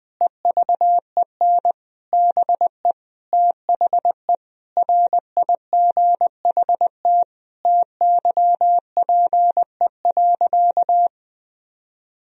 Morse code, heard as a keyed tone, about 20 wpm, 700 Hz. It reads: EVEN BE THE RIGHT TYPE.
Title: Sentences from Top 1000 Words - Review Part 3 - 20wpm